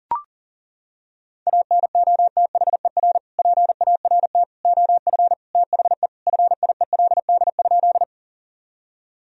Morse code, heard as a keyed tone, about 37 wpm, 700 Hz. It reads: ANOTHER PART OF THE FIELD?
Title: Sentences from Top 1000 Words - Review Part 2 - 40wpm